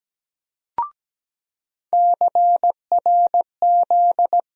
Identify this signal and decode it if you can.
Morse code, heard as a keyed tone, about 17 wpm, 700 Hz. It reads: CRZ
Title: USA Call Sign Postfixes - 3 Letters 17wpm